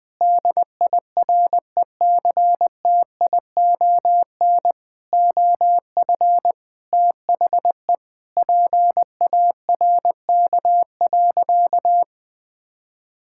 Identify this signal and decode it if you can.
Morse code, heard as a keyed tone, about 20 wpm, 700 Hz. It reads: DIRECTION OF THE PARK.
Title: Sentences with 5-Word Phrases - 20wpm